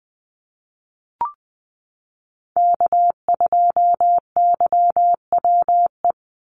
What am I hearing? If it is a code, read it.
Morse code, heard as a keyed tone, about 20 wpm, 700 Hz. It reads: K2YWE
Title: Call Signs Repeated 2x - 20wpm